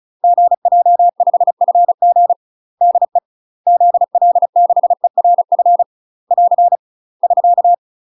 Morse code, heard as a keyed tone, about 34 wpm, 700 Hz. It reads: GJ5FG DE ZL6ERF <AR> <SK>
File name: UK MASTER QSOs TRAINING / 35WPM / UK MASTER QSOs 0000